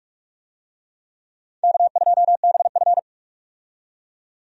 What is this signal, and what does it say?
Morse code, heard as a keyed tone, about 46 wpm, 700 Hz. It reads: K2BF
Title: USA Call Signs - 4 Characters - 45wpm